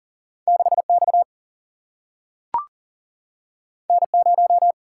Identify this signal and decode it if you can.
Morse code, heard as a keyed tone, about 40 wpm, 700 Hz. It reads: 6 X N 0
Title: Single-Letter Single-Number 40wpm